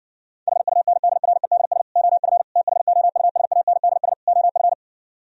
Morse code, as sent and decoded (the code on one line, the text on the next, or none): .... ..- -. -.. .-. -.. ... / --- ..-. / - .... --- ..- ... .- -. -.. ... / --- ..-.
HUNDREDS OF THOUSANDS OF